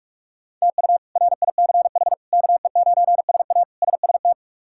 Morse code, heard as a keyed tone, about 46 wpm, 700 Hz. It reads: TU RICH KE9SA SST
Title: K1USN SST QSO Practice - 45wpm